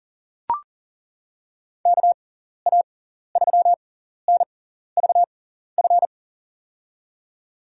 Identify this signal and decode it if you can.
Morse code, heard as a keyed tone, about 40 wpm, spaced as for 14 wpm, 700 Hz. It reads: KA3NVF
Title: USA Call Signs - 6 Characters - 6x Character Spacing - 40wpm